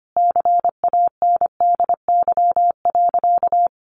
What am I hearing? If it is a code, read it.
Morse code, heard as a keyed tone, about 25 wpm, 700 Hz. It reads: CANDY.
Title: Easy Phrases 25wpm